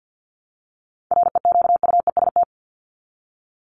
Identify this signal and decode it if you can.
Morse code, heard as a keyed tone, about 50 wpm, 700 Hz. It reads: REQUEST
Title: Top 1751-2000 Words 50wpm